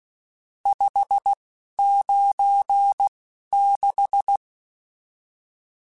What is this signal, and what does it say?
Morse code, heard as a keyed tone, about 16 wpm, spaced as for 11 wpm, 800 Hz. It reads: 596
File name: RSTs / RST-mp3s / Various RST @10